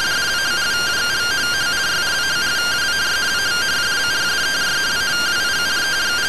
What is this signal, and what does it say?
Two-tone teleprinter signal, about 170 Hz shift.